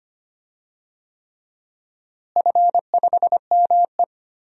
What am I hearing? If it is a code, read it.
Morse code, heard as a keyed tone, about 25 wpm, 700 Hz. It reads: F5ME
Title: DXCC-100 Call Signs - 4 Characters - Repeated 3x - 25wpm